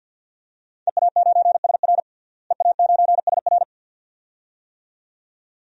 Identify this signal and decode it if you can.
Morse code, heard as a keyed tone, about 47 wpm, 700 Hz. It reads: EA9SR EA9SR
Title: DXCC-100 Call Signs - 5 Characters - Repeated 2x - 50wpm